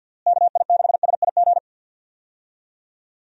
Morse code, heard as a keyed tone, about 50 wpm, 700 Hz. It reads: KI6SIG
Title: USA Call Signs - 6 Characters - Repeated 3x - 50wpm